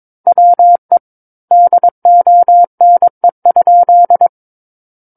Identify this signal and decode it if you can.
Morse code, heard as a keyed tone, about 22 wpm, 700 Hz. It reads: WE DONE?
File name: Book One / 13-Chapter 13 - 22 WPM